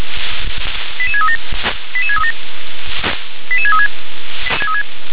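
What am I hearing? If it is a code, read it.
Five-tone selective call (ZVEI), heard as quick runs of five tones, tones of about 70 ms. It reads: 88637 88637 88636
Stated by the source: ZVEI five-tone selective call 88730 received by SDR.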